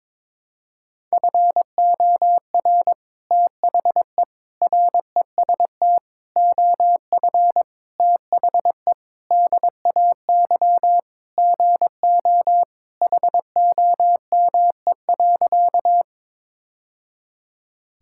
Morse code, heard as a keyed tone, about 22 wpm, 700 Hz. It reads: FOR THE REST OF THE DAY GO HOME.
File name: practice / 22wpm / Sentences with 3-Word Phrases - 22wpm